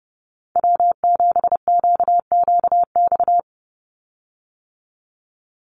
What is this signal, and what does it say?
Morse code, heard as a keyed tone, about 29 wpm, 700 Hz. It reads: W7QQX
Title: USA Call Signs 30wpm